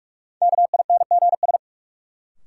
Morse code, heard as a keyed tone, about 46 wpm, 700 Hz. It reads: KINGS
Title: Sentences from Top 1000 Words - Review Part 6 - New Word Drill - 45wpm